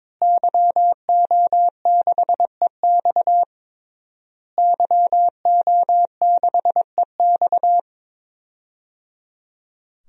Morse code, heard as a keyed tone, about 22 wpm, 700 Hz. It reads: YO6EX YO6EX
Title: DXCC-100 Call Signs - 5 Characters - Repeated 3x - 22wpm